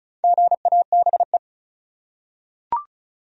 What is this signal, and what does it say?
Morse code, heard as a keyed tone, about 36 wpm, 700 Hz. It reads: GABE
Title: USA Common Names - Max 5 Letters - 35wpm